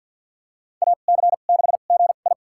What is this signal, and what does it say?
Morse code, heard as a keyed tone, about 52 wpm, spaced as for 36 wpm, 700 Hz. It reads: AC6GI
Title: USA Call Signs - 5 Characters - 2x Character Spacing - 50wpm